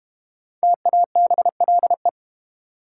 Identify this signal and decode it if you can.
Morse code, heard as a keyed tone, about 30 wpm, 700 Hz. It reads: TABLE
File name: Easy Words - Speed-Racing with Hints - 25wpm